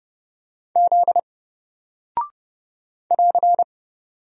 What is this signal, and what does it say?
Morse code, heard as a keyed tone, about 30 wpm, 700 Hz. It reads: Z <AR>
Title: Single Character (Letter:Number:Punctuation:Prosign) 30wpm